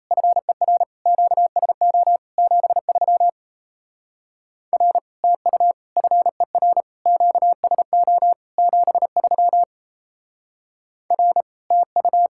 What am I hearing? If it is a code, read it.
Morse code, heard as a keyed tone, about 37 wpm, 700 Hz. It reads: FER QSO 73 R TU FER QSO 73 R TU